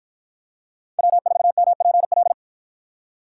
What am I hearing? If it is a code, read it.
Morse code, heard as a keyed tone, about 53 wpm, 700 Hz. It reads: W4MPL